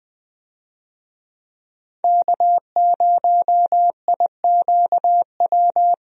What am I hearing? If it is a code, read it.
Morse code, heard as a keyed tone, about 20 wpm, 700 Hz. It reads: K0IQW
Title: USA Call Signs - 5 Characters - Repeated 2x - 20wpm